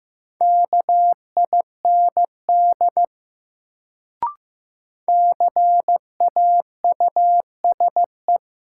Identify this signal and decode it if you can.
Morse code, heard as a keyed tone, about 15 wpm, 700 Hz. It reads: KIND CAUSE
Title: Sentences from Top 200 Words - New Word Drill - 15wpm